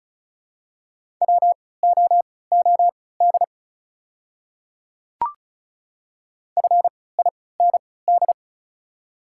Morse code, heard as a keyed tone, about 34 wpm, spaced as for 20 wpm, 700 Hz. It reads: WOOD FIND
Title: 3x Character Spacing - Top 300 Words - 35wpm